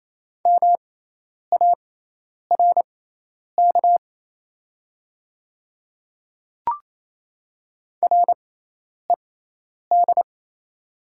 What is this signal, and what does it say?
Morse code, heard as a keyed tone, about 27 wpm, spaced as for 10 wpm, 690 Hz. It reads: MARK RED
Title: 6x Character Spacing - Top 500 Words - 28wpm